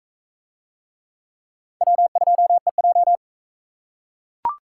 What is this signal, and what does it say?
Morse code, heard as a keyed tone, about 42 wpm, 700 Hz. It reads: W2EJ